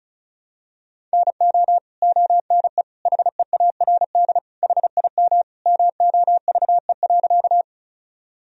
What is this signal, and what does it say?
Morse code, heard as a keyed tone, about 34 wpm, 700 Hz. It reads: NO ONE HEARD HIM MOVE.
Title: Sentences from Top 1000 Words - Review Part 1 - 35wpm